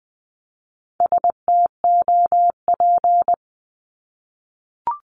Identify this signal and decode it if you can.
Morse code, heard as a keyed tone, about 20 wpm, 700 Hz. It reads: STOP
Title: Top 500 Words 20wpm